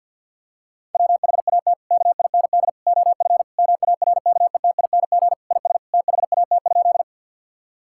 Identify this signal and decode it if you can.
Morse code, heard as a keyed tone, about 50 wpm, 700 Hz. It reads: WHAT KIND OF MARKETING IS THAT?
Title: Sentences from Top 1000 Words - Review Part 5 - 50wpm